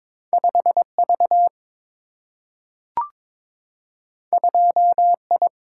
Morse code, heard as a keyed tone, about 22 wpm, 700 Hz. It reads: 5 V 2 I
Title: Single-Letter Single-Number 22wpm